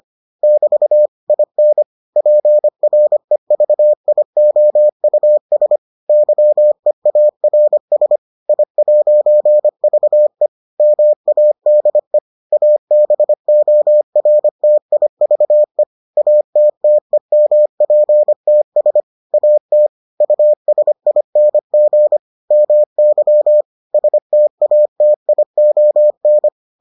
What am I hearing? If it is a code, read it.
Morse code, heard as a keyed tone, about 25 wpm, 600 Hz. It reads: <BT> IN PREVIOUS YEARS I'VE MADE ABORTIVE ATTEMPTS AT USING MY STATION